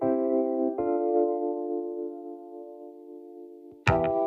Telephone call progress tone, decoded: dial tone